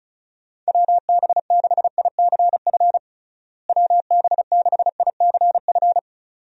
Morse code, heard as a keyed tone, about 36 wpm, 700 Hz. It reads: WB6ICF WB6ICF
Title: USA Call Signs - 6 Characters - Repeated 2x - 35wpm